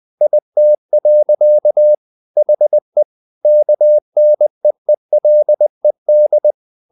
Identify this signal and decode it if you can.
Morse code, heard as a keyed tone, about 20 wpm, 600 Hz. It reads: IT. HE KNEELED
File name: the old man and the sea part 2 20wpm